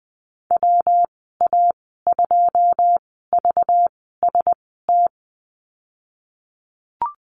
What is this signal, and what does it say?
Morse code, heard as a keyed tone, about 19 wpm, spaced as for 14 wpm, 700 Hz. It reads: WA2VST
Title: 193 Lesson 31g - Letter v - Call Signs 20wpm